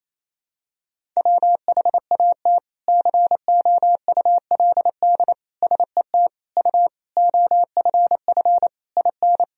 Morse code, heard as a keyed tone, about 28 wpm, 700 Hz. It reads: WHAT COULD SET U OFF IN
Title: Sentences from Top 200 Words and Abbreviations 28wpm